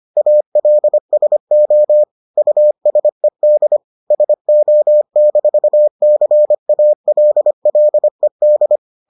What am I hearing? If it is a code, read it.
Morse code, heard as a keyed tone, about 25 wpm, 600 Hz. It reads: ALSO USED SO-CALLED